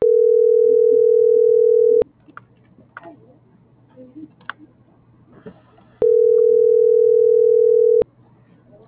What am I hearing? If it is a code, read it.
Ringback tone (North America), 440 + 480 Hz.